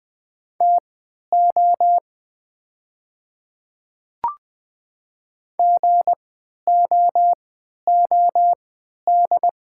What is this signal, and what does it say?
Morse code, heard as a keyed tone, about 21 wpm, spaced as for 11 wpm, 700 Hz. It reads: TO GOOD
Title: 3x Character Spacing - Top 200 Words - 20wpm